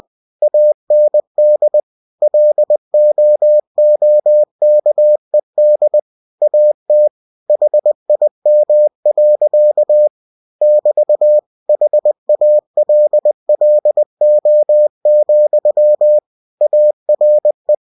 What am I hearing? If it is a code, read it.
Morse code, heard as a keyed tone, about 20 wpm, 600 Hz. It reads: AND LOOKED AT HIM. <BT> HALLO, ARE
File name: cw-books / winnie-the-pooh / winnie-20wpm-page-0016